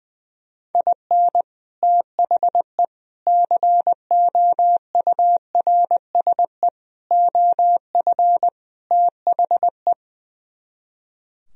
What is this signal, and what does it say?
Morse code, heard as a keyed tone, about 20 wpm, 700 Hz. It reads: IN THE COURSE OF THE
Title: Speed-Racing - Sets of 5 Words 20wpm